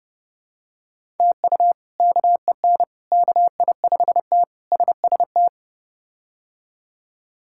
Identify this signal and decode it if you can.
Morse code, heard as a keyed tone, about 30 wpm, 700 Hz. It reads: TU KEN KI5T SST
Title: K1USN SST QSO Practice - 30wpm